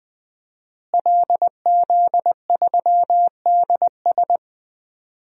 Morse code, heard as a keyed tone, about 19 wpm, 700 Hz. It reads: LZ3DS
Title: DXCC-100 Call Signs - 5 Characters - Repeated 2x - 20wpm